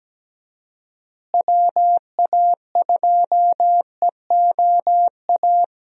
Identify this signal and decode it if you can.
Morse code, heard as a keyed tone, about 17 wpm, 700 Hz. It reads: WA2EOA
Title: USA Call Signs - 6 Characters - Repeated 3x - 17wpm